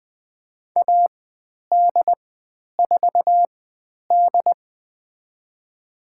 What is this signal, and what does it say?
Morse code, heard as a keyed tone, about 20 wpm, spaced as for 10 wpm, 700 Hz. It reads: AD4D
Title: 102 Lesson 17g - Letter u - Call Signs 20wpm